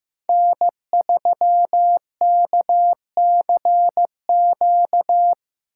Morse code, heard as a keyed tone, about 15 wpm, 700 Hz. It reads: N3KCQ
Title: USA Call Signs - 5 Characters - 8x Character Spacing - 15wpm